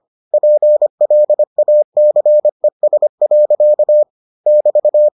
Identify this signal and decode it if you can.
Morse code, heard as a keyed tone, about 25 wpm, 600 Hz. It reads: PLACES. <BT>